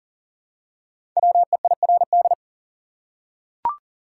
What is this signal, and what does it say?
Morse code, heard as a keyed tone, about 40 wpm, 700 Hz. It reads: WEIRD
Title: Five Letter Words - 40wpm